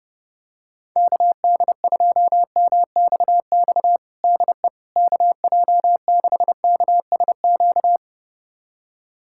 Morse code, heard as a keyed tone, about 30 wpm, 700 Hz. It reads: KD2MXX DE KJ6KSQ
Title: USA Callsign DE Callsign 30wpm